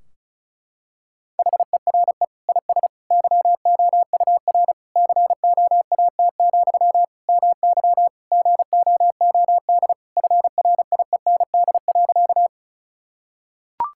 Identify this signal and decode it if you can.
Morse code, heard as a keyed tone, about 34 wpm, 700 Hz. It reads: HERE IS YOUR COAT, MY GOOD FRIEND.
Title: Sentences from Top 1600 Words 35wpm